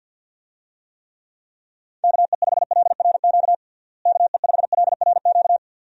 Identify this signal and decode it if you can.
Morse code, heard as a keyed tone, about 47 wpm, 700 Hz. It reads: KE5LRQ KE5LRQ
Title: USA Call Signs - 6 Characters - Repeated 2x - 50wpm